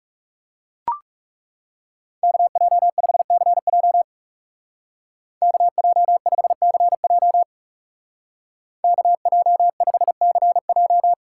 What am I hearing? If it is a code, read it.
Morse code, heard as a keyed tone, about 36 wpm, 700 Hz. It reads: KJ5CJ KJ5CJ KJ5CJ
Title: Speed Racing - USA Call Signs 30wpm